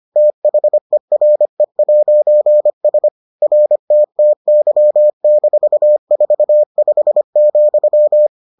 Morse code, heard as a keyed tone, about 25 wpm, 600 Hz. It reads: THERE'S RTTY-45,